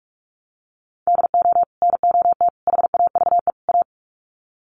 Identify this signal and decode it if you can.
Morse code, heard as a keyed tone, about 44 wpm, 700 Hz. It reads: DO NOT HAVE A
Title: Sets of 4 Words 45wpm